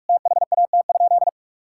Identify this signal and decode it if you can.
Morse code, heard as a keyed tone, about 41 wpm, 700 Hz. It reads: THAT?
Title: Sentences from Top 1000 Words - Review Part 6 - 45wpm